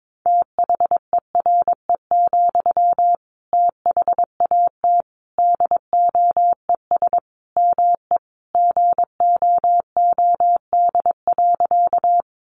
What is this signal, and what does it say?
Morse code, heard as a keyed tone, about 22 wpm, 700 Hz. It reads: THERE, THAT DOES ME GOOD.
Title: Sentences from Top 200 Words 22wpm